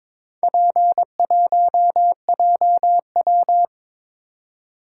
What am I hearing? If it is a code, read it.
Morse code, heard as a keyed tone, about 22 wpm, 700 Hz. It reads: P1JW
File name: practice / 22wpm / Four Letter-Number Combinations - 22wpm